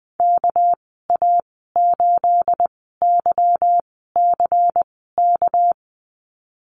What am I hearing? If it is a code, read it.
Morse code, heard as a keyed tone, about 20 wpm, spaced as for 15 wpm, 700 Hz. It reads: KA8YCK
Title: 245 Lesson 39g - Letter z - Call Signs 20wpm